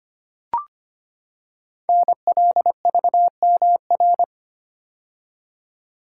Morse code, heard as a keyed tone, about 25 wpm, 700 Hz. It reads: NLVMR
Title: Code Groups - 5 Letters - 25wpm